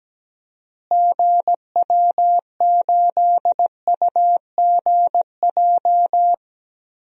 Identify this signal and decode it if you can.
Morse code, heard as a keyed tone, about 17 wpm, 700 Hz. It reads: GW8UGJ
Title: DXCC-100 Call Signs - 6 Characters - Repeated 3x - 17wpm